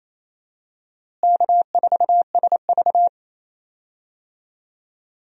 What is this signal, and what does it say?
Morse code, heard as a keyed tone, about 28 wpm, 700 Hz. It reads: K4SV
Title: USA Call Signs - 4 Characters - Repeated 2x - 28wpm